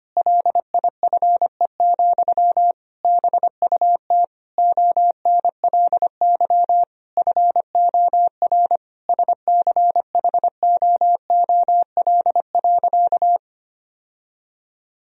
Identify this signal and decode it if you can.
Morse code, heard as a keyed tone, about 25 wpm, 700 Hz. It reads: LIFE, BUT ONLY FOR SCHOOL.
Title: Sentences from Top 1000 Words - Review Part 1 - 25wpm